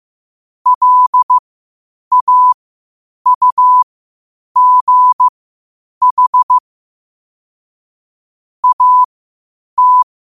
Morse code, heard as a keyed tone, about 15 wpm, spaced as for 8 wpm, 1000 Hz. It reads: LAUGH AT